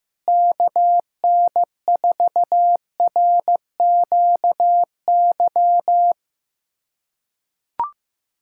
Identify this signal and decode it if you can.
Morse code, heard as a keyed tone, about 15 wpm, 700 Hz. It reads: KN4RQY
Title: USA Callsign DE Callsign - 6 Characters - 15wpm